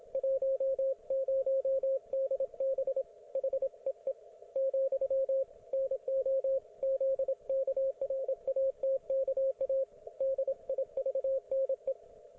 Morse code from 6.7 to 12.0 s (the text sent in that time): ZKRATKA DIVNE